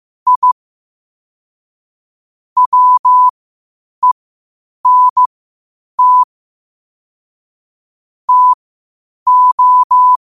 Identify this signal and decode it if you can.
Morse code, heard as a keyed tone, about 15 wpm, spaced as for 8 wpm, 1000 Hz. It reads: I WENT TO
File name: qotdfiles / 8wpm